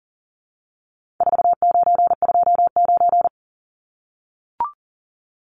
Morse code, heard as a keyed tone, about 40 wpm, 700 Hz. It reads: 4929
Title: Numbers - 4-Digit 40wpm